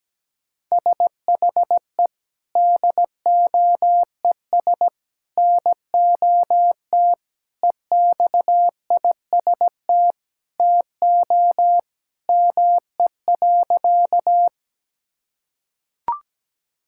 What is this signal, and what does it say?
Morse code, heard as a keyed tone, about 17 wpm, 700 Hz. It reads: SHE DOES NOT EXIST TO ME.
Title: Sentences from Top 1000 Words - Review Part 5 - 17wpm